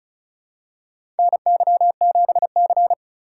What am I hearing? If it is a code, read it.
Morse code, heard as a keyed tone, about 36 wpm, 700 Hz. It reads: NY7C
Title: USA Call Signs - 4 Characters - 6x Character Spacing - 35wpm